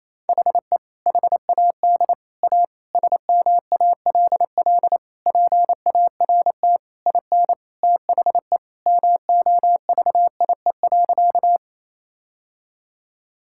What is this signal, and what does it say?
Morse code, heard as a keyed tone, about 28 wpm, 700 Hz. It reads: HE HAD A SMALL PART IN THE MOVIE.
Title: Sentences from Top 1000 Words - Review Part 5 - 28wpm